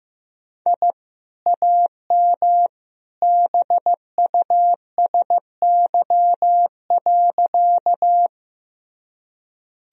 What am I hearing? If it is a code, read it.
Morse code, heard as a keyed tone, about 15 wpm, 700 Hz. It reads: I AM BUSY.
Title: Sentences from Top 1000 Words - Review Part 3 - 15wpm